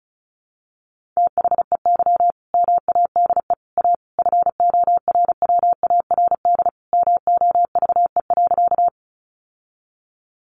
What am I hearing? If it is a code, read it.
Morse code, heard as a keyed tone, about 34 wpm, 700 Hz. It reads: THEY MADE A FORWARD MOVE.
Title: Sentences from Top 1300 Words 35wpm